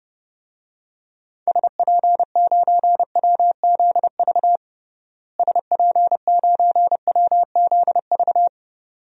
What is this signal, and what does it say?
Morse code, heard as a keyed tone, about 31 wpm, 700 Hz. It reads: SP9WZV SP9WZV
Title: DXCC-100 Call Signs - 6 Characters - Repeated 2x - 30wpm